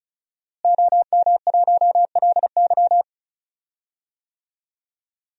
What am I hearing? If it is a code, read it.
Morse code, heard as a keyed tone, about 36 wpm, 700 Hz. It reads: OM1LY
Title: DXCC-100 Call Signs - 5 Characters - Repeated 3x - 35wpm